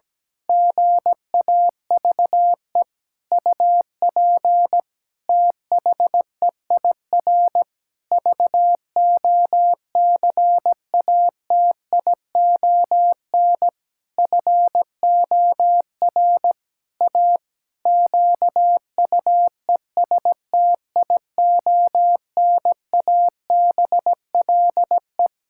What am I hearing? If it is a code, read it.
Morse code, heard as a keyed tone, about 17 wpm, 700 Hz. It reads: GAVE UP THEIR VOCATION FOR A QUESTIONABLE
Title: Pangram Challenge v2 - 17wpm